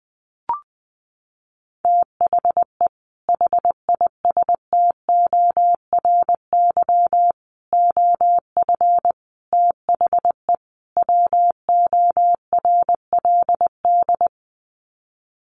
Morse code, heard as a keyed tone, about 20 wpm, 700 Hz. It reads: THE HISTORY OF THE WORLD
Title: Sets of 5 Words Encore2 20wpm